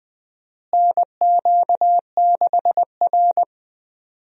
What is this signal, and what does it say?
Morse code, heard as a keyed tone, about 20 wpm, 700 Hz. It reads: NQ6R
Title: USA Call Signs - 4 Characters - Repeated 2x - 20wpm